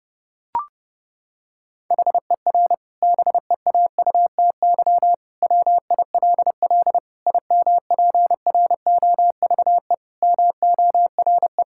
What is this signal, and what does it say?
Morse code, heard as a keyed tone, about 30 wpm, 700 Hz. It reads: HER BEAUTY WILL IMPROVE MORE
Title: Sentences from Top 1000 Words - Review Part 5 - 30wpm